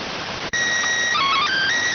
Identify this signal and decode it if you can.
SSTV (slow-scan television), PD 290.
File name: findings / CT AXIAL -256.gen-13 rx